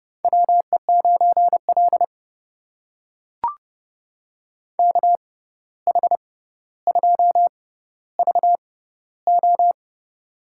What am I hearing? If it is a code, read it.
Morse code, heard as a keyed tone, about 30 wpm, spaced as for 14 wpm, 700 Hz. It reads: WE9L KH2VO
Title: USA Call Signs - 6x Character Spacing - 30wpm